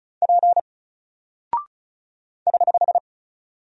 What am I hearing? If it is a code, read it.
Morse code, heard as a keyed tone, about 36 wpm, 700 Hz. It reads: P <HH>
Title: Single Character (Letter:Number:Punctuation:Prosign) 35wpm